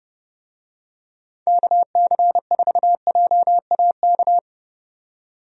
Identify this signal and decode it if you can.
Morse code, heard as a keyed tone, about 31 wpm, 700 Hz. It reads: KC4JAK
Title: USA Call Signs - 6 Characters - 30wpm